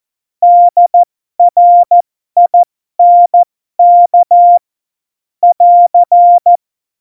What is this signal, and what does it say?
Morse code, heard as a keyed tone, about 14 wpm, spaced as for 12 wpm, 700 Hz. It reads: DRINK <AR>
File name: Morsekurs / Witze